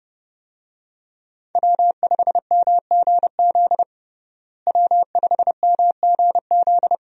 Morse code, heard as a keyed tone, about 30 wpm, 700 Hz. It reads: W5MGZ W5MGZ
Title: USA Call Signs - 5 Characters - Repeated 2x - 30wpm